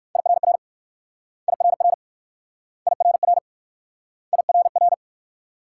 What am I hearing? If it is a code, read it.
Morse code, heard as a keyed tone, about 54 wpm, 700 Hz. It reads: IRR IRR IRR IRR
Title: Speed-Racing Multi-letter Phonemes - 45wpm